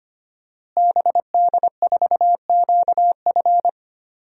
Morse code, heard as a keyed tone, about 25 wpm, 700 Hz. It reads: BD4QF
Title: DXCC-100 Call Signs - 5 Characters - Repeated 3x - 25wpm